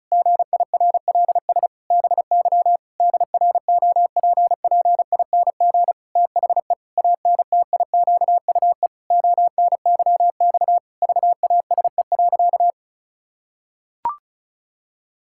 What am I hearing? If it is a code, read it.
Morse code, heard as a keyed tone, about 34 wpm, 700 Hz. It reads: GIRLS BY DROPPING THE ANTIQUE ONYX VASE.
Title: Pangram Challenge v2 - 35wpm